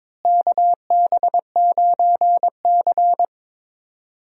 Morse code, heard as a keyed tone, about 22 wpm, 700 Hz. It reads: KB9C
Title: USA Call Signs - 4 Characters - Repeated 3x - 22wpm